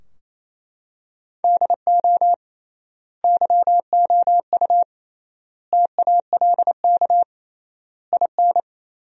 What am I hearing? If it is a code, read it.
Morse code, heard as a keyed tone, about 28 wpm, 700 Hz. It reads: DO YOU TALK IN
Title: 3x Word-Spacing - Sentences from Top 500 Words - 28wpm